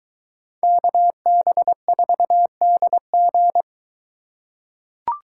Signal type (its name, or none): Morse code